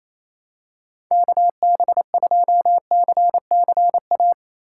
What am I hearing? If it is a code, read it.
Morse code, heard as a keyed tone, about 28 wpm, 700 Hz. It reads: KB2CCA
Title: USA Call Signs - 6 Characters - 4x Character Spacing - 28wpm